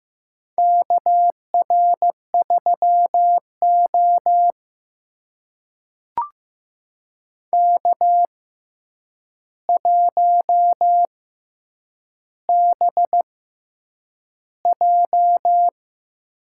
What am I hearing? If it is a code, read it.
Morse code, heard as a keyed tone, about 15 wpm, spaced as for 8 wpm, 700 Hz. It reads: KR3O K1BJ
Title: USA Call Signs - 4 Characters - 6x Character Spacing - 15wpm